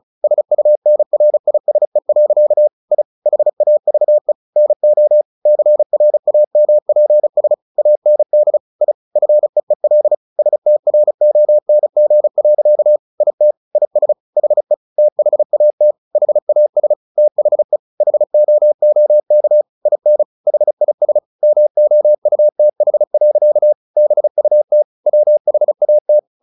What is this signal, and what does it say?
Morse code, heard as a keyed tone, about 34 wpm, 600 Hz. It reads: SUNRISE. I HAVE NO CRAMPS AND I FEEL STRONG. IT IS HE THAT HAS THE HOOK IN HIS MOUTH. BUT WHAT
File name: the old man and the sea / the old man and the sea 1 35wpm